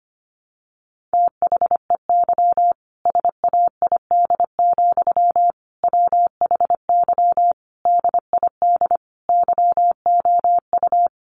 Morse code, heard as a keyed tone, about 25 wpm, 700 Hz. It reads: THEY SAID, WHY DID YOU
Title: Sentences from Top 1200 Words 25wpm